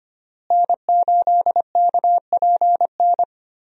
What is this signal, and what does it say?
Morse code, heard as a keyed tone, about 25 wpm, 700 Hz. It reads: N8KPN
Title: USA Call Signs - 5 Characters - Repeated 3x - 25wpm